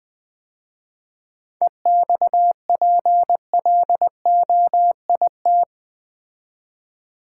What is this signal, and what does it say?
Morse code, heard as a keyed tone, about 20 wpm, 700 Hz. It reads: EXPLOIT
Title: 7 Letter Words - First 3 Letters Given - 20wpm